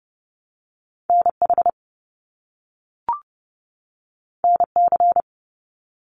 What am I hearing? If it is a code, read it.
Morse code, heard as a keyed tone, about 30 wpm, 700 Hz. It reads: NH NC
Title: US States 30wpm